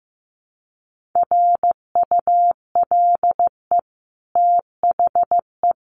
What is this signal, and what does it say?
Morse code, heard as a keyed tone, about 15 wpm, 700 Hz. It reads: RULE THE
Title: Sentences from Top 600 Words 15wpm